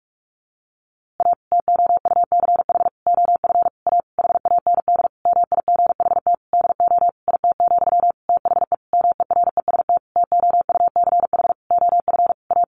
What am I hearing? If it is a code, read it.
Morse code, heard as a keyed tone, about 44 wpm, 700 Hz. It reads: A TOUCH OF A HAND MIGHT DO IT, THE MEREST TOUCH OF A
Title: A Sound Of Thunder 45wpm